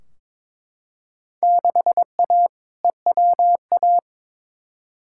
Morse code, heard as a keyed tone, about 22 wpm, 700 Hz. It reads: 6A EWA
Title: Field Day Exchanges Encore - 22wpm